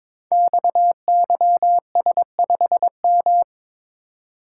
Morse code, heard as a keyed tone, about 22 wpm, 700 Hz. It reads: XYS5M
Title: Code Groups - 5 Letter:Numbers - 22wpm